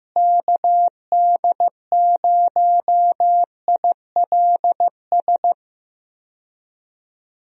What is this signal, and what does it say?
Morse code, heard as a keyed tone, about 16 wpm, 700 Hz. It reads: KD0ILS
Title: USA Call Signs - 6 Characters - Repeated 3x - 15wpm